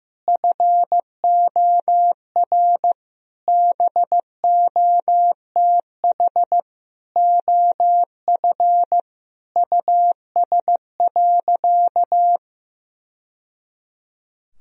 Morse code, heard as a keyed tone, about 15 wpm, 700 Hz. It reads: FOR BOTH OF US.
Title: Sentences from Top 1000 Words - Review Part 1 - 15wpm